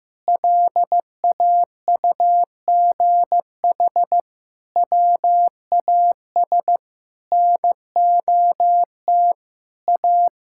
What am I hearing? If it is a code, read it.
Morse code, heard as a keyed tone, about 15 wpm, 700 Hz. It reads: LAUGH WAS NOT A
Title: Sentences from Top 1000 Words - Review Part 5 - 15wpm